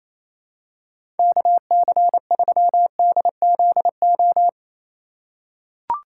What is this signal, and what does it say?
Morse code, heard as a keyed tone, about 28 wpm, 700 Hz. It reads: KC3DZO
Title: USA Call Signs - 3x Character Spacing - 28wpm